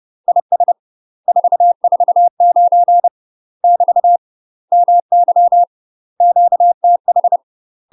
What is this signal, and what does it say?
Morse code, heard as a keyed tone, about 30 wpm, 700 Hz. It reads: IS 449 <BT> MY QTH
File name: UK MASTER QSOs TRAINING / 30WPM / UK MASTER QSOs 0043